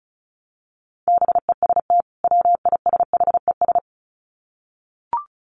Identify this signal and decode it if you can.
Morse code, heard as a keyed tone, about 34 wpm, 700 Hz. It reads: BEST WISHES
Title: Common QSO Phrases 35wpm